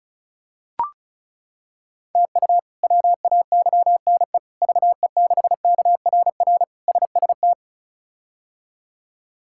Morse code, heard as a keyed tone, about 34 wpm, 700 Hz. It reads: TU WAYNE VE6KRR SST
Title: K1USN SST QSO Practice - 35wpm